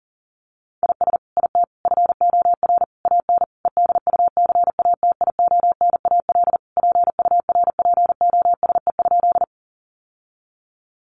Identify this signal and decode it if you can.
Morse code, heard as a keyed tone, about 40 wpm, 700 Hz. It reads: IS IT FOR AN EDUCATIONAL PURPOSE?
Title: Sentences from Top 1200 Words 40wpm